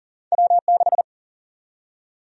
Morse code, heard as a keyed tone, about 40 wpm, 700 Hz. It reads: W6
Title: USA Call Sign Prefixes - 40wpm